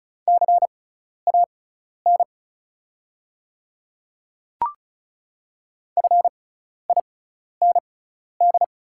Morse code, heard as a keyed tone, about 36 wpm, spaced as for 12 wpm, 700 Hz. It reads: CAN FIND
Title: 6x Character Spacing - Top 200 Words - 35wpm